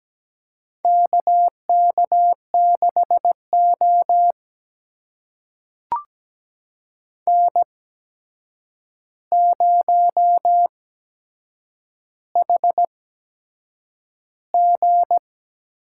Morse code, heard as a keyed tone, about 17 wpm, spaced as for 7 wpm, 700 Hz. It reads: KK6O N0HG
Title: USA Call Signs - 4 Characters - 8x Character Spacing - 17wpm